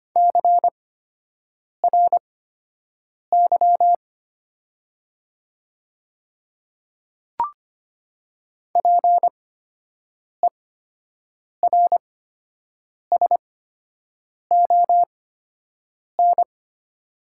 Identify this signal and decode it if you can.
Morse code, heard as a keyed tone, about 25 wpm, spaced as for 7 wpm, 690 Hz. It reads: CRY PERSON
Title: 8x Character Spacing - Top 500 Words - 25wpm